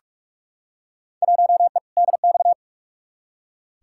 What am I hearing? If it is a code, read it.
Morse code, heard as a keyed tone, about 46 wpm, 700 Hz. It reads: 1E DX
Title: Field Day Exchanges Encore - 45wpm